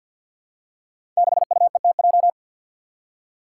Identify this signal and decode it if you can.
Morse code, heard as a keyed tone, about 49 wpm, 700 Hz. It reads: 6UETJ
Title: Five Challenging Letter-Number Combinations 50wpm